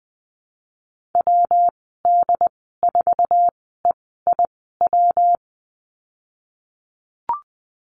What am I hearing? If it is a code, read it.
Morse code, heard as a keyed tone, about 20 wpm, spaced as for 14 wpm, 700 Hz. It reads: WD4EIW
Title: 140 Lesson 23g - Letter f - Call Signs 20wpm